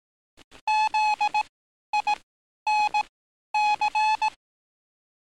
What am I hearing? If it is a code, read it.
Morse code, heard as a keyed tone, about 18 wpm, spaced as for 11 wpm, 850 Hz. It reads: ZINC
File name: training / NAVPERS 11049to11060 / NAVPERS 11051 RA RB